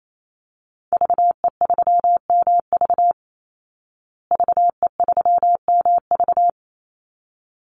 Morse code, heard as a keyed tone, about 28 wpm, 700 Hz. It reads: VE3MV VE3MV
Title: Call Signs Repeated 3x - 28wpm